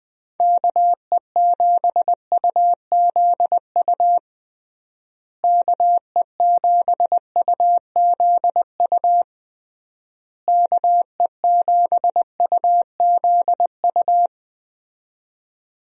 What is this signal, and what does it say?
Morse code, heard as a keyed tone, about 20 wpm, 700 Hz. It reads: KE7UZU KE7UZU KE7UZU
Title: USA Call Signs - 6 Characters - Repeated 3x - 20wpm